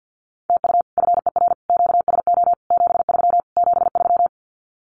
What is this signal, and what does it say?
Morse code, heard as a keyed tone, about 48 wpm, 700 Hz. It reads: TU FER QSO 73 73
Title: Common QSO Phrases Encore 50wpm